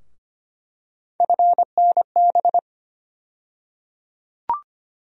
Morse code, heard as a keyed tone, about 25 wpm, 700 Hz. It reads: FNB
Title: USA Call Sign Postfixes - 3 Letters 25wpm